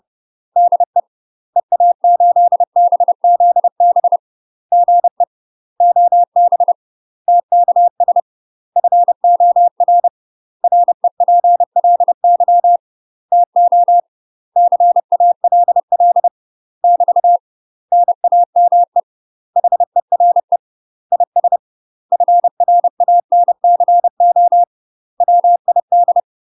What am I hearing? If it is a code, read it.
Morse code, heard as a keyed tone, about 30 wpm, 700 Hz. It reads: DE EA8BZB GE OB TKS FOR REPLY TO CALL <BT> NAME HERE IS FRANCO WID
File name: UK MASTER QSOs TRAINING / 30WPM / UK MASTER QSOs 0023